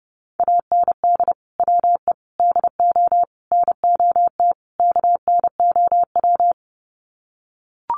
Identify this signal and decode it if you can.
Morse code, heard as a keyed tone, about 30 wpm, 700 Hz. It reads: AND WE DO NOT KNOW
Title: Sets of 5 Words Encore 30wpm